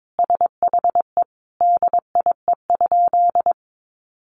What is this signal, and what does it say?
Morse code, heard as a keyed tone, about 22 wpm, 700 Hz. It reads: SHE DIE?
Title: Sentences from Top 1300 Words 22wpm